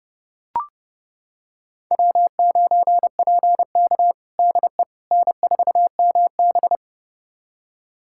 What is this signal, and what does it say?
Morse code, heard as a keyed tone, about 30 wpm, 700 Hz. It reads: W9PK DE N4MB
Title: USA Callsign DE Callsign - 4 Characters - 30wpm